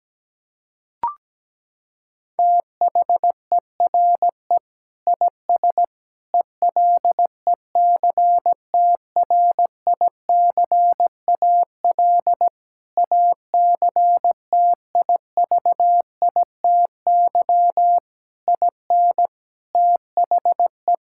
Morse code, heard as a keyed tone, about 17 wpm, 700 Hz. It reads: THERE IS ELECTRICAL ACTIVITY IN THE
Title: Sentences from Top 1000 Words - Review Part 6 - 17wpm